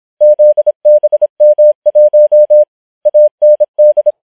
Morse code, heard as a keyed tone, about 26 wpm, 600 Hz. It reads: ZBM1 AND
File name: challenge193